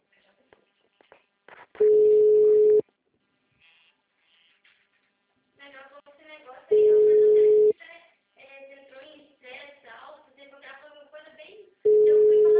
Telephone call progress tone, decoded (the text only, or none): ringback tone